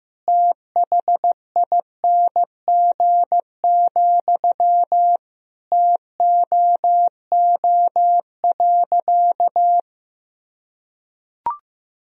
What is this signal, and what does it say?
Morse code, heard as a keyed tone, about 16 wpm, 700 Hz. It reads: THING, TOO.
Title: Sentences from Top 200 Words and Abbreviations 15wpm